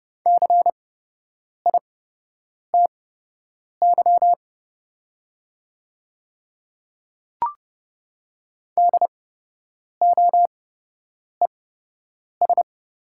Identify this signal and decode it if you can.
Morse code, heard as a keyed tone, about 30 wpm, spaced as for 8 wpm, 700 Hz. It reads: CITY DOES
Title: 8x Character Spacing - Top 300 Words - 30wpm